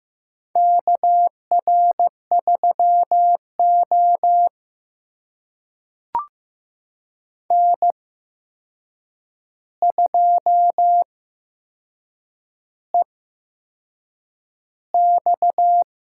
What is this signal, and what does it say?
Morse code, heard as a keyed tone, about 15 wpm, spaced as for 6 wpm, 700 Hz. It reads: KR3O N2EX
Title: USA Call Signs - 4 Characters - 8x Character Spacing - 15wpm